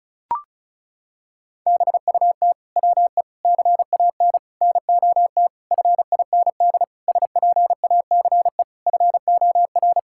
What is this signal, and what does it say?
Morse code, heard as a keyed tone, about 36 wpm, 700 Hz. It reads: BUT WE CAN NOT FIND SPACE FOR
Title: Sentences from Top 1000 Words - Review Part 5 - 35wpm